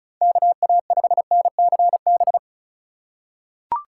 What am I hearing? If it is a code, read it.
Morse code, heard as a keyed tone, about 34 wpm, 700 Hz. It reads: KA5NCB
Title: USA Call Signs - 6 Characters - 2x Character Spacing - 35wpm